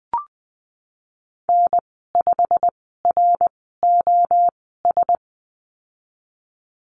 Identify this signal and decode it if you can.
Morse code, heard as a keyed tone, about 20 wpm, spaced as for 15 wpm, 700 Hz. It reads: N5ROS
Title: 088 Lesson 15g - Number 5 - Call Signs 20wpm